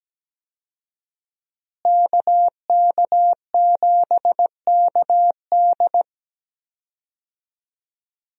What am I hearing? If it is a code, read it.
Morse code, heard as a keyed tone, about 17 wpm, 700 Hz. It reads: KK7KD
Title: USA Call Signs - 5 Characters - Repeated 3x - 17wpm